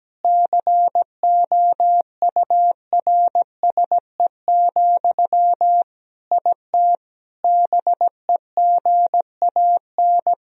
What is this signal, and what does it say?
Morse code, heard as a keyed tone, about 17 wpm, 700 Hz. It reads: COURSE, IT BEGAN
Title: Sentences from Top 1000 Words - Review Part 1 - 17wpm